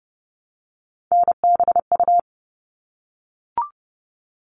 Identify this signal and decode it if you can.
Morse code, heard as a keyed tone, about 30 wpm, 700 Hz. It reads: NBU
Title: All Cut Numbers - 3-Digit 30wpm